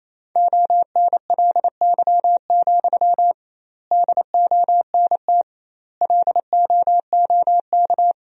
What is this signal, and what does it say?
Morse code, heard as a keyed tone, about 28 wpm, 700 Hz. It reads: ONLY, DONT LOOK
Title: 2x Word-Spacing - Sentences from Top 300 Words - 28wpm